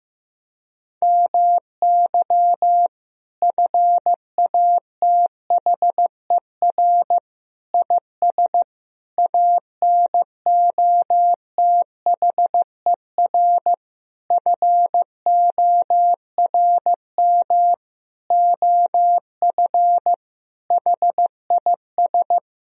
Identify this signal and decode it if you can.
Morse code, heard as a keyed tone, about 15 wpm, 700 Hz. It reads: MY FATHER IS ANOTHER FORM OF HIS
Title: Sentences from Top 1000 Words - Review Part 1 - 15wpm